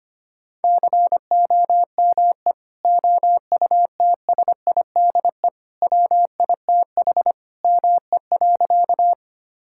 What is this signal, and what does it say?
Morse code, heard as a keyed tone, about 25 wpm, 700 Hz. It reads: COME OUTSIDE WITH ME.
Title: Sentences from Top 1000 Words - Review Part 6 - 25wpm